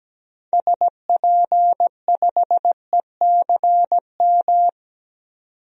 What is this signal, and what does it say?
Morse code, heard as a keyed tone, about 17 wpm, 700 Hz. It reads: SP5ECM
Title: DXCC-100 Call Signs - 6 Characters - 17wpm